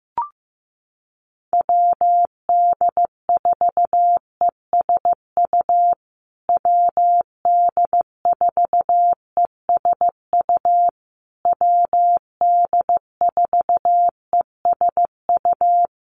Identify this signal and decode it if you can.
Morse code, heard as a keyed tone, about 15 wpm, 700 Hz. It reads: WD4ESU WD4ESU WD4ESU
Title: USA Call Signs Repeated 3x 15wpm